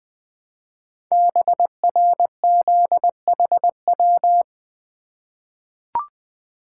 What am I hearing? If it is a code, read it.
Morse code, heard as a keyed tone, about 20 wpm, 700 Hz. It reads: BRZHW
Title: Code Groups - 5 Letters - 20wpm